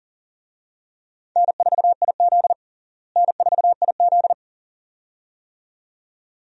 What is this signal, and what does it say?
Morse code, heard as a keyed tone, about 37 wpm, 700 Hz. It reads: N4IZ N4IZ
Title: USA Call Signs - 4 Characters - Repeated 2x - 40wpm